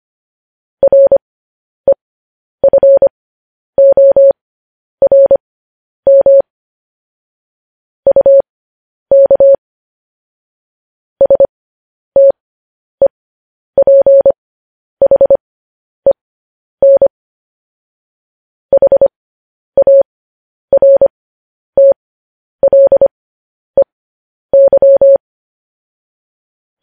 Morse code, heard as a keyed tone, about 25 wpm, spaced as for 10 wpm, 550 Hz. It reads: REFORM UK STEPHEN HARTLEY
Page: news headlines in morse code at 10 wpm